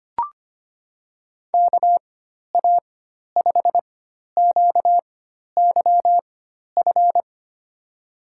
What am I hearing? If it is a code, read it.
Morse code, heard as a keyed tone, about 25 wpm, spaced as for 12 wpm, 700 Hz. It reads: KA5QYF
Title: USA Call Signs - 6 Characters - 4x Character Spacing - 25wpm